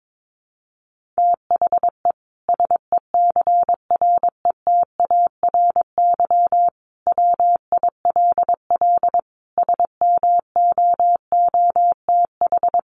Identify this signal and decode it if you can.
Morse code, heard as a keyed tone, about 22 wpm, 700 Hz. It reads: THE SECRETARY WILL SMOOTH
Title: Sentences from Top 1400 Words 22wpm